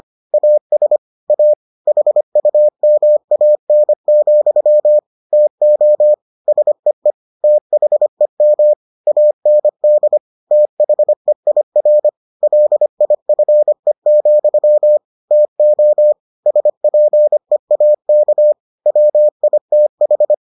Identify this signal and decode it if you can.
Morse code, heard as a keyed tone, about 25 wpm, 600 Hz. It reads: AS A HUMAN, TO SEE THEM AND THEIR LIFE, TO SPEAK WITH